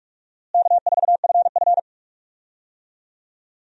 Morse code, heard as a keyed tone, about 46 wpm, 700 Hz. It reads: K4FF
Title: USA Call Signs - 4 Characters - Repeated 3x - 45wpm